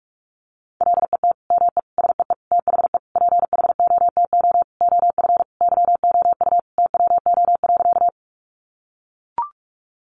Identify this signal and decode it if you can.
Morse code, heard as a keyed tone, about 44 wpm, 700 Hz. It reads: LET ME SEE THE PHOTO OF YOU TWO.
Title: Sentences from Top 1200 Words 45wpm